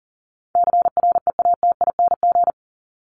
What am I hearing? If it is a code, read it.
Morse code, heard as a keyed tone, about 40 wpm, 700 Hz. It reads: CREATING
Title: ING Suffix - Top 100 Words with Base Words 40wpm